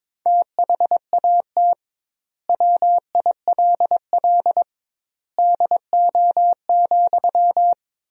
Morse code, heard as a keyed tone, about 22 wpm, 700 Hz. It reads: THAT WILL DO,
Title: 2x Word-Spacing - Sentences from Top 100 Words - 22wpm